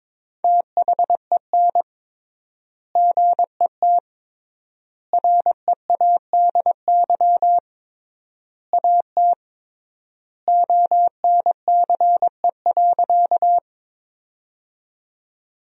Morse code, heard as a keyed tone, about 22 wpm, 700 Hz. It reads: THEN GET READY AT ONCE.
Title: 3x Word-Spacing - Sentences from Top 400 Words - 22wpm